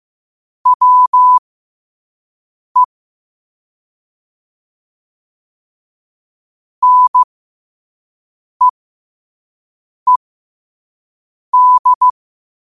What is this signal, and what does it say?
Morse code, heard as a keyed tone, about 15 wpm, spaced as for 5 wpm, 1000 Hz. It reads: WE NEED
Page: qotd-at-5-wpm